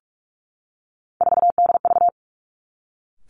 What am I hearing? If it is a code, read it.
Morse code, heard as a keyed tone, about 46 wpm, 700 Hz. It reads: 4DV
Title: Three Challenging Letter-Number Combinations 45wpm